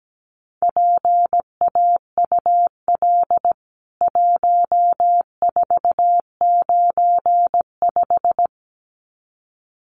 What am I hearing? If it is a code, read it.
Morse code, heard as a keyed tone, about 17 wpm, 700 Hz. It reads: PAUL 1495
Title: CWT Exchange 17wpm